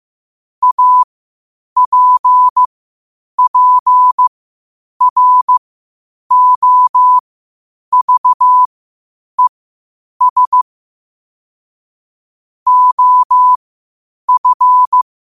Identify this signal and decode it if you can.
Morse code, heard as a keyed tone, about 15 wpm, spaced as for 8 wpm, 1000 Hz. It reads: APPROVES OF
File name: qotdfiles / 8wpm